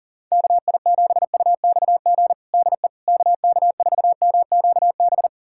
Morse code, heard as a keyed tone, about 40 wpm, 700 Hz. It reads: KI7UXG DE KK4MQB
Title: USA Callsign DE Callsign - 6 Characters - 40wpm